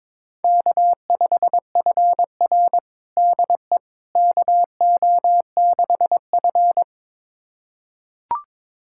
Morse code, heard as a keyed tone, about 22 wpm, 700 Hz. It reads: K5FR DE KO6F
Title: USA Callsign DE Callsign - 4 Characters - 22wpm